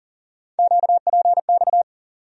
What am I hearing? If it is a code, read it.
Morse code, heard as a keyed tone, about 40 wpm, 700 Hz. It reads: QPX
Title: USA Call Sign Postfixes - 3 Letters 40wpm